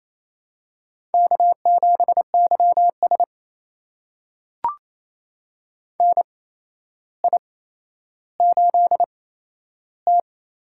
Morse code, heard as a keyed tone, about 27 wpm, spaced as for 12 wpm, 700 Hz. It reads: K7YS NI8T
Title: USA Call Signs - 4 Characters - 8x Character Spacing - 28wpm